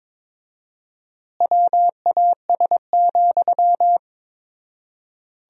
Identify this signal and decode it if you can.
Morse code, heard as a keyed tone, about 22 wpm, 690 Hz. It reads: WAS,
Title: 4x Word-Spacing - Sentences from Top 400 Words - 22wpm